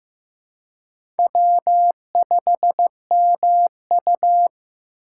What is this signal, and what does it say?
Morse code, heard as a keyed tone, about 15 wpm, 700 Hz. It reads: W5MU
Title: USA Call Signs - 4 Characters - 4x Character Spacing - 15wpm